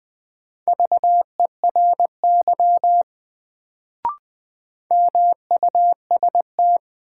Morse code, heard as a keyed tone, about 21 wpm, 700 Hz. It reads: VERY MUST
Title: Sets of 2 Words - New Word Drill - 20wpm